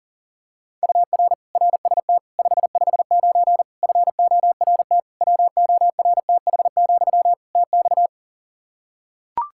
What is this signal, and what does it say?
Morse code, heard as a keyed tone, about 40 wpm, 700 Hz. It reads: UR RST 559 FORT WORTH, TX
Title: Practice QSO - Exchange 1 - Beginning - 40wpm